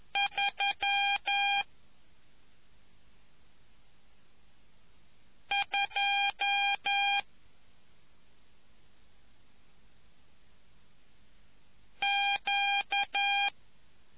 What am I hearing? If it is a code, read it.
Morse code, heard as a keyed tone, about 11 wpm, 2400 Hz. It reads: 3 2 Q